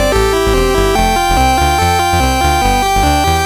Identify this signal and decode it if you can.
There is no code.